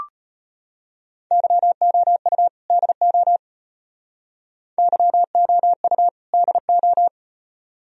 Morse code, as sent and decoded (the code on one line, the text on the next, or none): -.-- --- ..- / -.. --- / -.-- --- ..- / -.. ---
YOU DO YOU DO